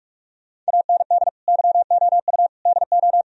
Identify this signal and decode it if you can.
Morse code, heard as a keyed tone, about 58 wpm, 700 Hz. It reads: AND YOU DO